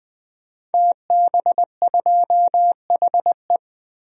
Morse code, heard as a keyed tone, about 21 wpm, 700 Hz. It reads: TB2HE
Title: DXCC-100 Call Signs - 5 Characters - Repeated 3x - 20wpm